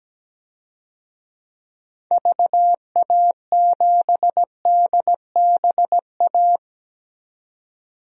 Morse code, heard as a keyed tone, about 17 wpm, 700 Hz. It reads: VA7DBA
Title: DXCC-100 Call Signs - 6 Characters - 17wpm